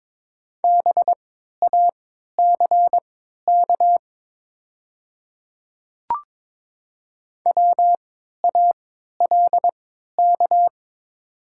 Morse code, heard as a keyed tone, about 22 wpm, spaced as for 12 wpm, 700 Hz. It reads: BACK WALK
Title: 3x Character Spacing - Top 300 Words - 22wpm